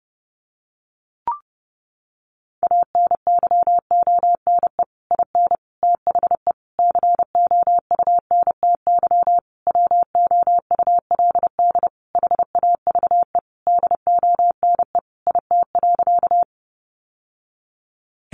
Morse code, heard as a keyed tone, about 30 wpm, 700 Hz. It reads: ANYONE IN THE COUNTY WOULD HAVE DONE IT.
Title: Sentences from Top 1200 Words 30wpm